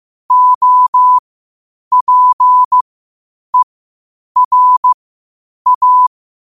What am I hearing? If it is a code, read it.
Morse code, heard as a keyed tone, about 15 wpm, spaced as for 8 wpm, 1000 Hz. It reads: OPERA
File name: qotdfiles / 8wpm